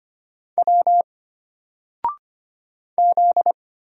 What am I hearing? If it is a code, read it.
Morse code, heard as a keyed tone, about 25 wpm, 700 Hz. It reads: W Z
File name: Single-Letter-Number-25wpm